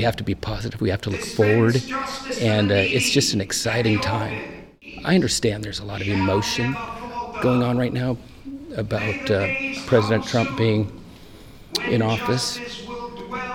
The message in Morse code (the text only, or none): none